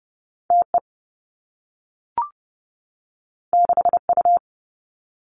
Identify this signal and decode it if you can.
Morse code, heard as a keyed tone, about 30 wpm, 700 Hz. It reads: T E 6 U
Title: Two Challenging Letter-Number Combinations 30wpm